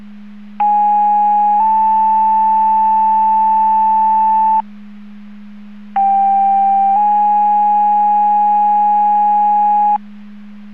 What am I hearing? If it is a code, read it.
Two-tone page, heard as a one-second tone, then a three-second tone, about 830 then 880 Hz, about 790 then 830 Hz. It is heard from 0.6 to 10.0 s.